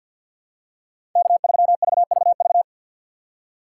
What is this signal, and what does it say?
Morse code, heard as a keyed tone, about 52 wpm, 700 Hz. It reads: K3VVV
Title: USA Call Signs - 5 Characters - 2x Character Spacing - 50wpm